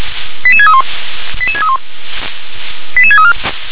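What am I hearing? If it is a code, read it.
Five-tone selective call (ZVEI), heard as quick runs of five tones, tones of about 70 ms. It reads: 88631 88631 88634
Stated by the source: ZVEI five-tone selective call, received by SDR.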